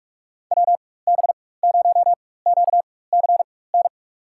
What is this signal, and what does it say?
Morse code, heard as a keyed tone, about 45 wpm, spaced as for 21 wpm, 700 Hz. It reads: WB0QCN
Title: USA Call Signs - 6 Characters - 4x Character Spacing - 45wpm